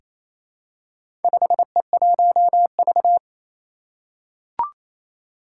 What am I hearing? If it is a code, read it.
Morse code, heard as a keyed tone, about 28 wpm, 700 Hz. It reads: 5E1V
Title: Four Challenging Letter-Number Combinations 28wpm